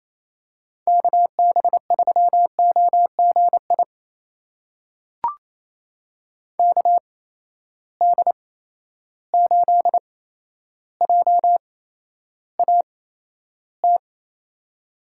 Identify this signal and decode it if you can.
Morse code, heard as a keyed tone, about 28 wpm, spaced as for 12 wpm, 700 Hz. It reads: KB3OGI KD8JAT
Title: USA Call Signs - 6 Characters - 8x Character Spacing - 28wpm